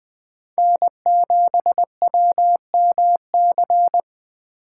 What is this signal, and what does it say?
Morse code, heard as a keyed tone, about 21 wpm, 700 Hz. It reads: N7WMC
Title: Speed Racing - USA 1x3 Call Signs 17wpm